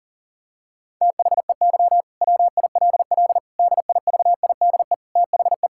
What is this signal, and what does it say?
Morse code, heard as a keyed tone, about 40 wpm, 700 Hz. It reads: THEY WILL DIVIDE THE
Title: Sentences from Top 1600 Words 40wpm